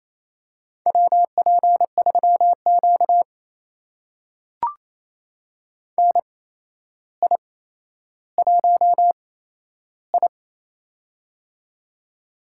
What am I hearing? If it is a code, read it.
Morse code, heard as a keyed tone, about 28 wpm, spaced as for 12 wpm, 700 Hz. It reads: WP3Q NI1I
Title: USA Call Signs - 4 Characters - 8x Character Spacing - 28wpm